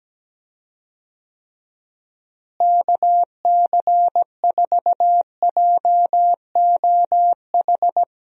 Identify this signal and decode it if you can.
Morse code, heard as a keyed tone, about 17 wpm, 700 Hz. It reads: KC4JOH
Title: USA Call Signs - 6 Characters - Repeated 2x - 17wpm